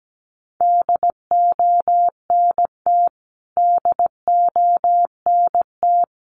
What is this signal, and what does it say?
Morse code, heard as a keyed tone, about 17 wpm, 700 Hz. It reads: DONT DONT
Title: Top 500 Words Repeated 17wpm